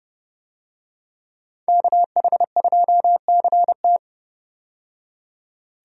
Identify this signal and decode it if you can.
Morse code, heard as a keyed tone, about 30 wpm, 700 Hz. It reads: KH2CT
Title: USA Call Signs - 5 Characters - 30wpm